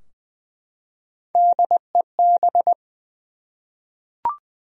Morse code, heard as a keyed tone, about 20 wpm, 700 Hz. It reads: DEB
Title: USA Common Names - Max 5 Letters - 20wpm